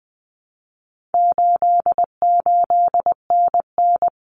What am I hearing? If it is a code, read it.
Morse code, heard as a keyed tone, about 20 wpm, 700 Hz. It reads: 88NN
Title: Common Cut Numbers - 4-Digit 20wpm